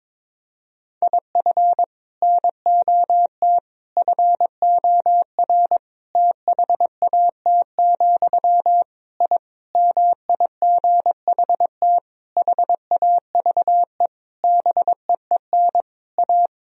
Morse code, heard as a keyed tone, about 22 wpm, 700 Hz. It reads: IF NOT FOR THAT, I MIGHT HAVE BEEN A
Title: Sentences from Top 1000 Words - Review Part 3 - 22wpm